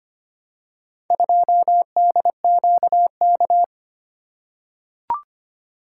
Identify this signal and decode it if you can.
Morse code, heard as a keyed tone, about 25 wpm, 700 Hz. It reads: 2DQK
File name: practice / 25wpm / Four Letter-Number Combinations - 25wpm